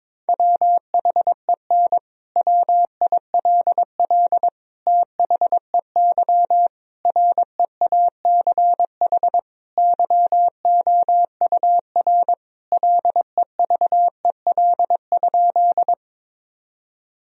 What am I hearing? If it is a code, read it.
Morse code, heard as a keyed tone, about 22 wpm, 700 Hz. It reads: WHEN WILL THEY REACH YOUR LEVEL?
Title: Sentences from Top 1000 Words - Review Part 4 - 22wpm